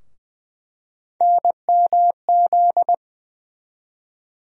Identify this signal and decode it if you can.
Morse code, heard as a keyed tone, about 21 wpm, 700 Hz. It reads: NMZ
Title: USA Call Sign Postfixes - 3 Letters 20wpm